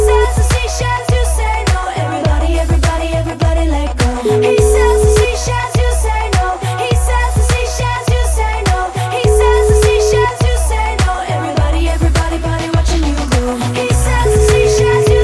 Ringback tone (Europe, 425 Hz), with noise.